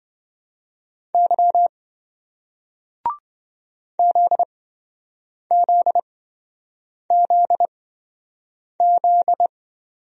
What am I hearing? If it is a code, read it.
Morse code, heard as a keyed tone, about 26 wpm, spaced as for 7 wpm, 700 Hz. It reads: Y ZZZZ